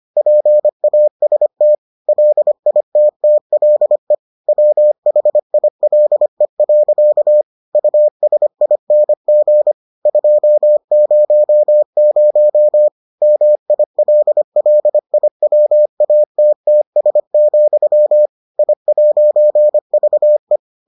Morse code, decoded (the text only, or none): PAST LITTLE WHILE. USING 200 MILLIWATTS, I'VE